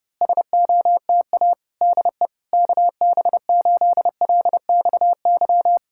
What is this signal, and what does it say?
Morse code, heard as a keyed tone, about 28 wpm, 700 Hz. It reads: SOTA DE KB8LXY
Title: Speed-Racing Common QSO Phrases Encore - 25wpm